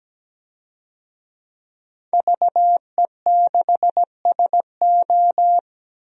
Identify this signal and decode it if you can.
Morse code, heard as a keyed tone, about 17 wpm, 700 Hz. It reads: VE6SO
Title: DXCC-100 Call Signs - 5 Characters - Repeated 3x - 17wpm